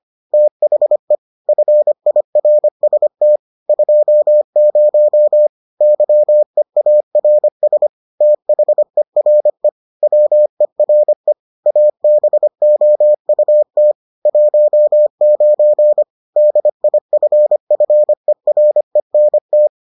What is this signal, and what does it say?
Morse code, heard as a keyed tone, about 25 wpm, 600 Hz. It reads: THE FIRST 20 YEARS THERE WERE ABOUT 19 DIFFERENT